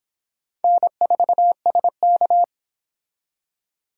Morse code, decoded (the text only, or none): N4SK